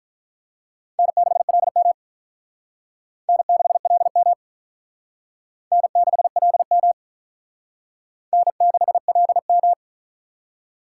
Morse code, heard as a keyed tone, about 45 wpm, 700 Hz. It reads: N6LM N6LM N6LM N6LM